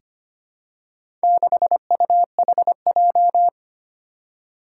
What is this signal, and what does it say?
Morse code, heard as a keyed tone, about 25 wpm, 700 Hz. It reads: 6UHJ
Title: Four Challenging Letter-Number Combinations 25wpm